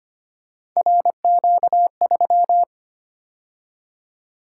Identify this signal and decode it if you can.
Morse code, heard as a keyed tone, about 28 wpm, 700 Hz. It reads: RQ3
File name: Speed-Racing - Three Letter-Number Combinations - 25wpm